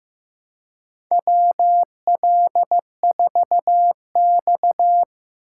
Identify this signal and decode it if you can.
Morse code, heard as a keyed tone, about 15 wpm, 700 Hz. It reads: WL4X
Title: USA Call Signs - 8x Character Spacing - 15wpm